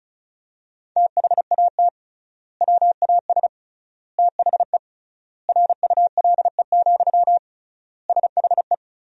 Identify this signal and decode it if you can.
Morse code, heard as a keyed tone, about 36 wpm, 700 Hz. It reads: THAT WAS THE RULE, SHE
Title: 3x Word-Spacing - Sentences from Top 400 Words - 35wpm